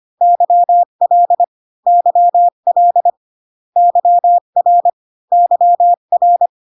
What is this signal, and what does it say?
Morse code, heard as a keyed tone, about 25 wpm, 700 Hz. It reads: YL YL YR YR